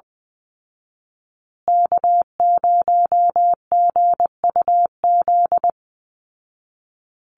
Morse code, heard as a keyed tone, about 20 wpm, 700 Hz. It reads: K0GUZ
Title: Call Signs Repeated 2x - 20wpm